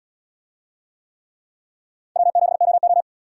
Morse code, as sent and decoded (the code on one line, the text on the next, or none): .-- --... --- ---
W7OO